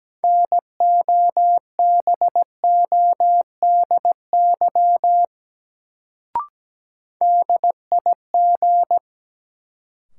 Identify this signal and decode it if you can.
Morse code, heard as a keyed tone, about 17 wpm, 700 Hz. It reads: NOBODY DIG
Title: Sentences from Top 1600 Words - New Word Drill Alt - 17wpm